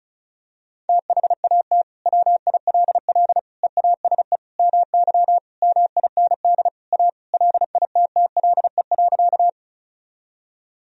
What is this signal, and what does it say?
Morse code, heard as a keyed tone, about 36 wpm, 700 Hz. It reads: THAT WILL EASE MY MIND A LITTLE.
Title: Sentences from Top 1000 Words - Review Part 1 - 35wpm